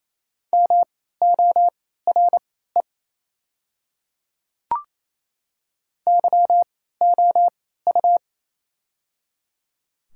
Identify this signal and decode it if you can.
Morse code, heard as a keyed tone, about 28 wpm, spaced as for 16 wpm, 700 Hz. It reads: MORE YOU
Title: 3x Character Spacing - Top 100 Words - 28wpm